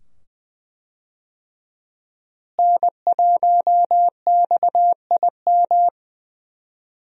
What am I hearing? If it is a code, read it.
Morse code, heard as a keyed tone, about 20 wpm, 700 Hz. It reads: N1XIM
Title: USA Call Signs - 5 Characters - Repeated 2x - 20wpm